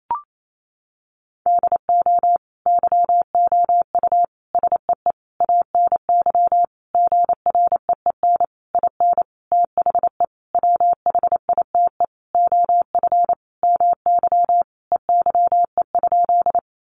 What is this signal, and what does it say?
Morse code, heard as a keyed tone, about 28 wpm, 700 Hz. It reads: DO YOU SEE ANY GREEN IN THE WHITE OF MY EYE?
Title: Sentences from Top 500 Words 28wpm